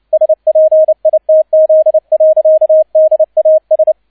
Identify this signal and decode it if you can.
Morse code, heard as a keyed tone, about 29 wpm, 630 Hz, with radio noise. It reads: SPITZ. DAS
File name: 002 Zweiter Streich 30wpm